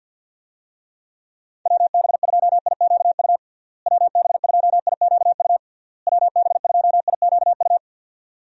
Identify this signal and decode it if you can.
Morse code, heard as a keyed tone, about 52 wpm, 700 Hz. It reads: WB2IQU WB2IQU WB2IQU
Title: USA Call Signs - 6 Characters - Repeated 3x - 50wpm